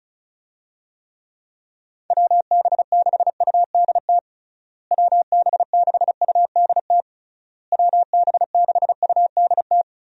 Morse code, heard as a keyed tone, about 34 wpm, 700 Hz. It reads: WB6UDT WB6UDT WB6UDT
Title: USA Call Signs - 6 Characters - Repeated 3x - 35wpm